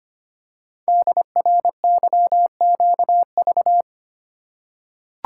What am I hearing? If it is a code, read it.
Morse code, heard as a keyed tone, about 25 wpm, 700 Hz. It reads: DRYQV
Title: Code Groups - 5 Letters - 25wpm